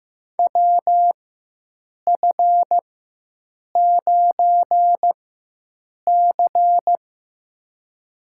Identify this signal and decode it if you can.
Morse code, heard as a keyed tone, about 15 wpm, spaced as for 7 wpm, 700 Hz. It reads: WF9C
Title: USA Call Signs - 4x Character Spacing - 15wpm